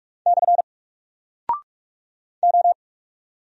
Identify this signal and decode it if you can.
Morse code, heard as a keyed tone, about 44 wpm, 700 Hz. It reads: / O
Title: Single Character (Letter:Number:Punctuation:Prosign) 45wpm